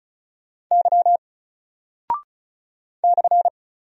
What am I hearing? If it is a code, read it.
Morse code, heard as a keyed tone, about 36 wpm, 700 Hz. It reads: Y /
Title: Single Character (Letter:Number:Punctuation:Prosign) 35wpm